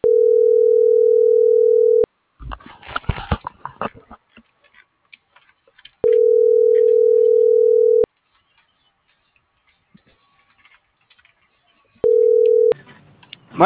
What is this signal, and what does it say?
Ringback tone (North America), 440 + 480 Hz.